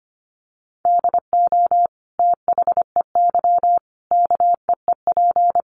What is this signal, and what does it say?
Morse code, heard as a keyed tone, about 25 wpm, 700 Hz. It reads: DO THEY KEEP
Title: Sentences from Top 1200 Words 25wpm